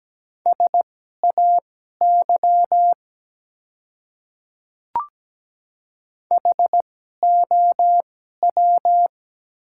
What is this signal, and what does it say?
Morse code, heard as a keyed tone, about 17 wpm, spaced as for 12 wpm, 700 Hz. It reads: SAY HOW
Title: 2x Character Spacing - Top 200 Words - 17wpm